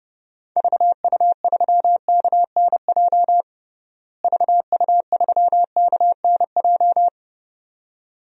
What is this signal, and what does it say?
Morse code, heard as a keyed tone, about 29 wpm, 700 Hz. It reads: VU3KNJ VU3KNJ
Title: DXCC-100 Call Signs - 6 Characters - Repeated 2x - 30wpm